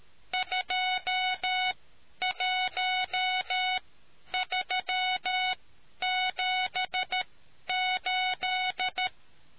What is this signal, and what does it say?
Morse code, heard as a keyed tone, about 13 wpm, spaced as for 10 wpm, 740 Hz. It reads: 21378